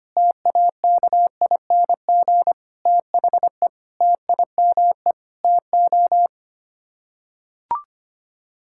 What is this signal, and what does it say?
Morse code, heard as a keyed tone, about 25 wpm, 700 Hz. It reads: TAKING THE TIME TO
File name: Sets of 9 Words - 25wpm